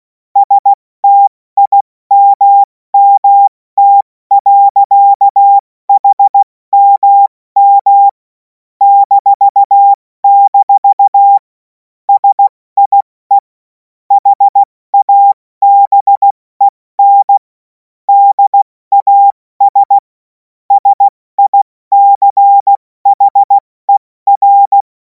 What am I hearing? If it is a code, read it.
Morse code, heard as a keyed tone, about 16 wpm, 800 Hz. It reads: STIMMT. HMM -- SIE HABEN DAS SICHER
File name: Morsekurs / Witze / Witz 02. Juni 2026 70 F80